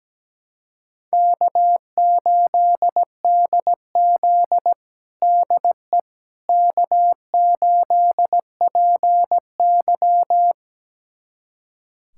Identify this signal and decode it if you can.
Morse code, heard as a keyed tone, about 17 wpm, 700 Hz. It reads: K8DZ DE K8PY
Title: USA Callsign DE Callsign - 4 Characters - 17wpm